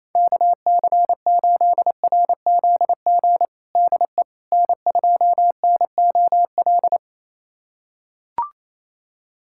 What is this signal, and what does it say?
Morse code, heard as a keyed tone, about 28 wpm, 700 Hz. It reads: KC8RZG DE N2NOL
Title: USA Callsign DE Callsign 28wpm